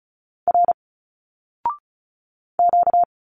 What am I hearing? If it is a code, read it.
Morse code, heard as a keyed tone, about 33 wpm, 700 Hz. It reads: R Q
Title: Single Letters 35wpm